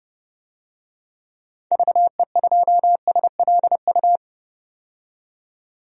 Morse code, heard as a keyed tone, about 31 wpm, 700 Hz. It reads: VE2SLU
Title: DXCC-100 Call Signs - 6 Characters - 30wpm